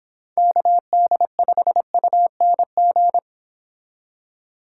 Morse code, heard as a keyed tone, about 26 wpm, 700 Hz. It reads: KD5UNG